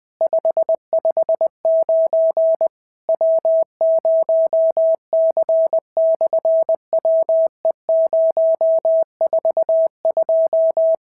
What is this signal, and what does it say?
Morse code, heard as a keyed tone, about 20 wpm, 690 Hz. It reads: 559 W0C/WE042
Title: SOTA QSO Practice 20wpm